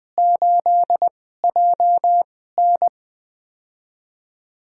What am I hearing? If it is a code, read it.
Morse code, heard as a keyed tone, about 20 wpm, spaced as for 14 wpm, 700 Hz. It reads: 8JN
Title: USA Call Signs - 2x Character Spacing - 20wpm